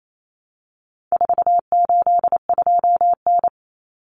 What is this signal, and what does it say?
Morse code, heard as a keyed tone, about 28 wpm, 700 Hz. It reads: 482N
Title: Common Cut Numbers - 4-Digit 28wpm